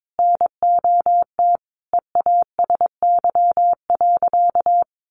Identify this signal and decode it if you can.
Morse code, heard as a keyed tone, about 22 wpm, 700 Hz. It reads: NOT EASY.
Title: Sentences from Top 1500 Words 22wpm